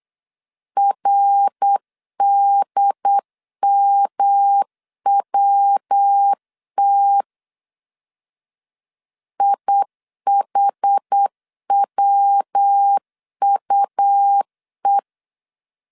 Morse code, heard as a keyed tone, about 8 wpm, 800 Hz. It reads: RDMWT IHWUE
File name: CW / DET / 1 3 1500C S B 8wpm 4141